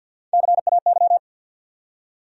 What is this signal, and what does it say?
Morse code, heard as a keyed tone, about 48 wpm, 700 Hz. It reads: CAY
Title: USA Call Sign Postfixes - 3 Letters 50wpm